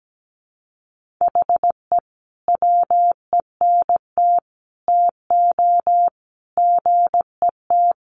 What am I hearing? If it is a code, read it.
Morse code, heard as a keyed tone, about 17 wpm, 700 Hz. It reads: HE WENT TO GET
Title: Sentences from Top 1500 Words 17wpm